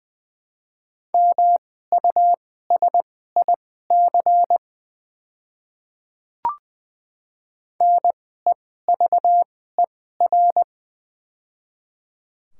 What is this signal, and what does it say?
Morse code, heard as a keyed tone, about 20 wpm, spaced as for 14 wpm, 700 Hz. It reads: MUSIC NEVER
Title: 2x Character Spacing - Top 300 Words - 20wpm